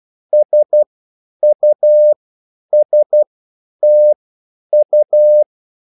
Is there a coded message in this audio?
Yes: Morse code from 0.3 to 5.4 s, heard as a keyed tone, about 12 wpm, spaced as for 9 wpm, 600 Hz. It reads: SUSTU